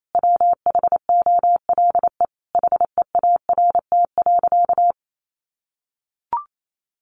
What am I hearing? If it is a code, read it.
Morse code, heard as a keyed tone, about 28 wpm, 700 Hz. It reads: WHOLE HEART.
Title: Sentences from Top 1200 Words 28wpm